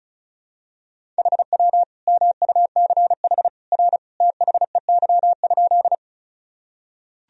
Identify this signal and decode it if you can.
Morse code, heard as a keyed tone, about 34 wpm, 700 Hz. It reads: HW MUCH R THEY?
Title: Sentences from Top 200 Words and Abbreviations 35wpm